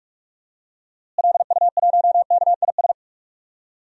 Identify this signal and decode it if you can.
Morse code, heard as a keyed tone, about 46 wpm, 700 Hz. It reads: LU1KIS
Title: DXCC-100 Call Signs - 6 Characters - Repeated 2x - 45wpm